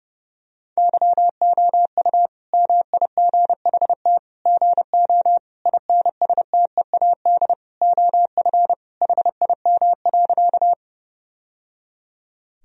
Morse code, heard as a keyed tone, about 30 wpm, 700 Hz. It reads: YOU MIGHT GO INSTEAD OF HIM.
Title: Sentences from Top 1000 Words - Review Part 4 - 30wpm